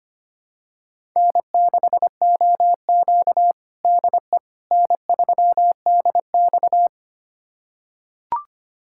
Morse code, heard as a keyed tone, about 25 wpm, 700 Hz. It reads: N6OQ DE N3DX
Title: USA Callsign DE Callsign - 4 Characters - 25wpm